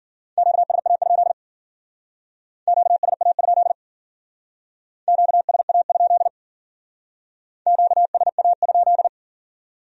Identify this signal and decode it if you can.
Morse code, heard as a keyed tone, about 48 wpm, 700 Hz. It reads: QSA? QSA? QSA? QSA?
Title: Speed-Racing Q-Codes - 40wpm